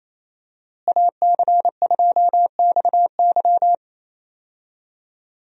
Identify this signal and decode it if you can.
Morse code, heard as a keyed tone, about 27 wpm, 700 Hz. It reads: AC2XY
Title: USA Call Signs - 5 Characters - Repeated 2x - 28wpm